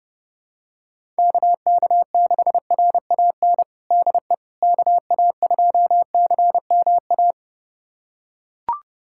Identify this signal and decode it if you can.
Morse code, heard as a keyed tone, about 32 wpm, 700 Hz. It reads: KK6RAN DE KA2CMA
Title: USA Callsign DE Callsign - 6 Characters - 30wpm